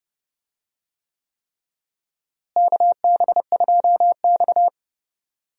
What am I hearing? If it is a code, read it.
Morse code, heard as a keyed tone, about 29 wpm, 700 Hz. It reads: KB2X
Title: USA Call Signs - 4 Characters - Repeated 3x - 30wpm